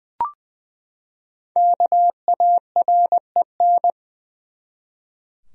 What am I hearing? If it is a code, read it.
Morse code, heard as a keyed tone, about 20 wpm, 700 Hz. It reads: KAREN
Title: USA Common Names - Max 5 Letters - 20wpm